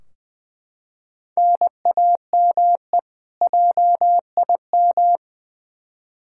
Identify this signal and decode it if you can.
Morse code, heard as a keyed tone, about 20 wpm, 700 Hz. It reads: NAME JIM
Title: K1USN SST QSO Practice 20wpm